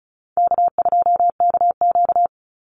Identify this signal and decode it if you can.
Morse code, heard as a keyed tone, about 34 wpm, 700 Hz. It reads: K2KQ
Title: Call Signs - 35wpm